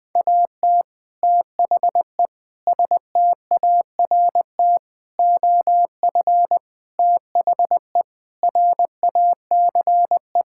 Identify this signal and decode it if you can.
Morse code, heard as a keyed tone, about 20 wpm, 700 Hz. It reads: AT THE START OF THE RACE
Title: Sentences with 5-Word Phrases - 20wpm